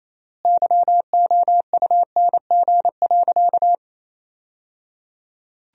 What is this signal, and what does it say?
Morse code, heard as a keyed tone, about 28 wpm, 700 Hz. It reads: YOUNG.
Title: 3x Word-Spacing - Sentences from Top 400 Words - 28wpm